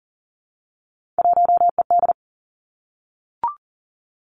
Morse code, heard as a keyed tone, about 38 wpm, 700 Hz. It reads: 1ED
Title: Three Challenging Letter-Number Combinations 40wpm